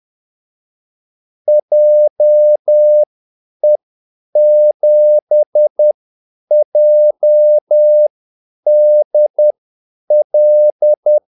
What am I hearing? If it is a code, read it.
Morse code, heard as a keyed tone, about 10 wpm, spaced as for 8 wpm, 600 Hz. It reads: JE7JDL
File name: Callsigns-10-8wpm